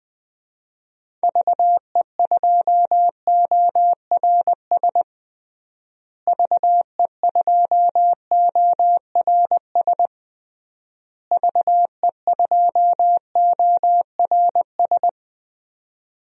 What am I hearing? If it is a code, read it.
Morse code, heard as a keyed tone, about 20 wpm, 700 Hz. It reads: VE2ORS VE2ORS VE2ORS
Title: DXCC-100 Call Signs - 6 Characters - Repeated 3x - 20wpm